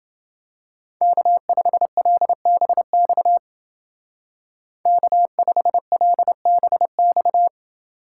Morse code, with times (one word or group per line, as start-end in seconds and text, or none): 1.0-3.4 K5LBX
4.9-7.5 K5LBX